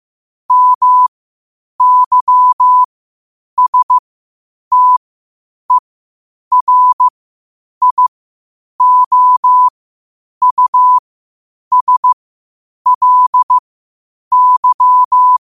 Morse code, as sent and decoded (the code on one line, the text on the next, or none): -- -.-- ... - . .-. .. --- ..- ... .-.. -.--
MYSTERIOUSLY